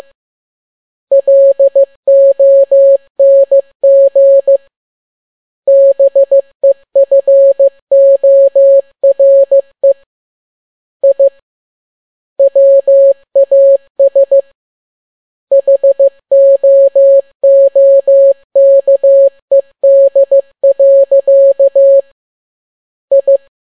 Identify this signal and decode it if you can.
Morse code, heard as a keyed tone, about 15 wpm, 550 Hz. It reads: LONG BEFORE I WAS HOOKED. I
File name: n6tt p1 15C13